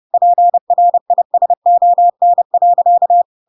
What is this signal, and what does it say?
Morse code, heard as a keyed tone, about 30 wpm, 700 Hz. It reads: PRISON.
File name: Pinocchio / Pinocchio ch3 30 30 ik1hge